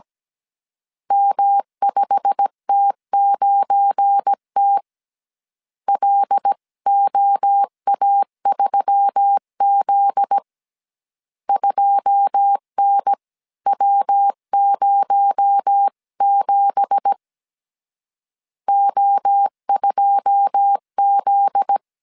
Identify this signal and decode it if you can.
Morse code, heard as a keyed tone, about 17 wpm, 800 Hz. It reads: M5T9T LOA3Z 2NW07 O2Z
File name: CW / DET / 1 9 1500C S C 16wpm 2909